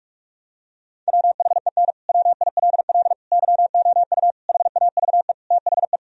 Morse code, heard as a keyed tone, about 41 wpm, 700 Hz. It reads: WHEN WILL YOU HAVE THE
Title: Sentences from Top 1000 Words - Review Part 3 - 45wpm